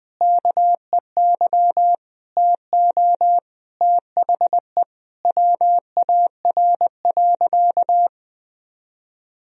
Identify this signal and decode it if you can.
Morse code, heard as a keyed tone, about 20 wpm, 700 Hz. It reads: KEY TO THE WAR.
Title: Sentences from Top 1000 Words - Review Part 4 - 20wpm